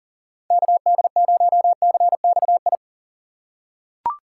Morse code, heard as a keyed tone, about 39 wpm, 700 Hz. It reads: KD0CXI
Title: USA Call Signs - 2x Character Spacing - 40wpm